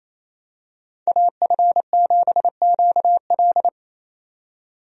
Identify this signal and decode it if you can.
Morse code, heard as a keyed tone, about 28 wpm, 700 Hz. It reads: AF7QL
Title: USA Call Signs - 5 Characters - 28wpm